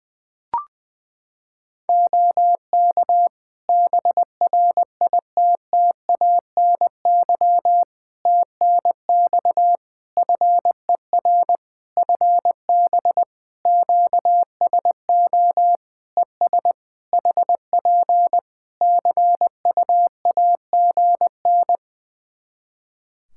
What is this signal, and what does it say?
Morse code, heard as a keyed tone, about 20 wpm, 700 Hz. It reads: OK BRITTANY TNX FER FB QSO ES HP CUAGN
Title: K1USN SST QSO Practice 20wpm